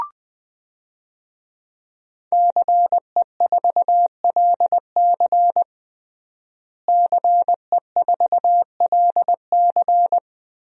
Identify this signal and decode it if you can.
Morse code, heard as a keyed tone, about 20 wpm, 700 Hz. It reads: CE4LC CE4LC
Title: DXCC-100 Call Signs - 5 Characters - Repeated 2x - 20wpm